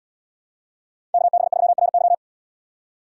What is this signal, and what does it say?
Morse code, heard as a keyed tone, about 77 wpm, 700 Hz. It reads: KB3RY